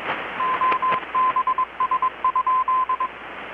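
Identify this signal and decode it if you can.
Morse code, heard as a keyed tone, about 22 wpm, 1030 Hz, with radio noise. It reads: OBS?
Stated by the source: Morse from KPH's Night of Nights XX maritime radiotelegraphy event, received in Chiapas, Mexico about 3,800 km away on a Kenwood R-600 with a grounded 10 m long-wire antenna.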